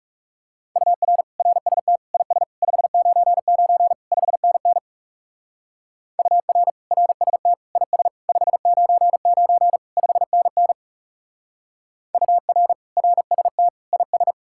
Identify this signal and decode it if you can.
Morse code, heard as a keyed tone, about 35 wpm, 700 Hz. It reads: UR RST IS 599 5NN UR RST IS 599 5NN UR RST IS
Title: Speed-Racing Common QSO Phrases - 30wpm